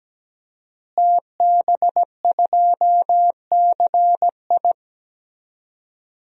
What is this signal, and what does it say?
Morse code, heard as a keyed tone, about 17 wpm, 700 Hz. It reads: TB2CI
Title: DXCC-100 Call Signs - 5 Characters - Repeated 2x - 17wpm